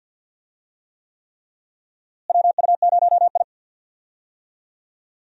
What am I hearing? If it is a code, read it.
Morse code, heard as a keyed tone, about 49 wpm, 700 Hz. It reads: WU0I
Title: USA Call Signs - 4 Characters - Repeated 3x - 50wpm